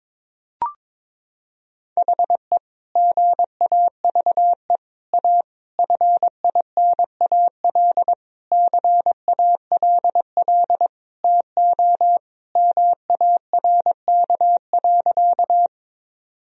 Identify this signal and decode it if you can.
Morse code, heard as a keyed tone, about 22 wpm, 700 Hz. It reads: HE GAVE A FINAL CALL TO MARK.
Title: Sentences from Top 1000 Words - Review Part 2 - 22wpm